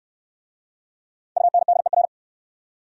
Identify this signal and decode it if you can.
Morse code, heard as a keyed tone, about 68 wpm, 700 Hz. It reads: UNDER